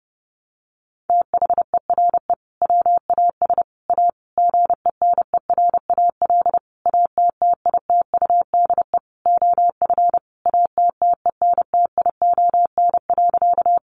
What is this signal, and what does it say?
Morse code, heard as a keyed tone, about 30 wpm, 700 Hz. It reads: THERE WAS A GENERAL ATTITUDE OF ATTENTION.
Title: Sentences from Top 1300 Words 30wpm